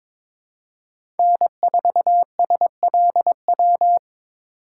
Morse code, heard as a keyed tone, about 21 wpm, 700 Hz. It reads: N4SLW
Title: USA Call Signs - 5 Characters - 3x Character Spacing - 22wpm